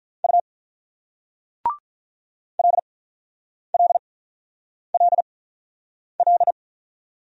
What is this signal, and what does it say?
Morse code, heard as a keyed tone, about 46 wpm, spaced as for 9 wpm, 690 Hz. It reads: U LLLL